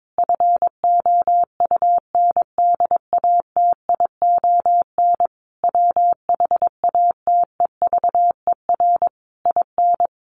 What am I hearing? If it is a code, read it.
Morse code, heard as a keyed tone, about 22 wpm, 700 Hz. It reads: FOUNDATION WHATEVER IN
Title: Sentences from Top 1200 Words 22wpm